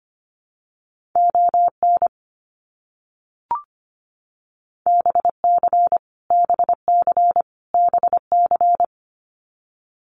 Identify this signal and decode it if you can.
Morse code, heard as a keyed tone, about 25 wpm, 700 Hz. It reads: ON BC BC BC
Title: Canadian Provinces and Territories - Repeated 3x - 25wpm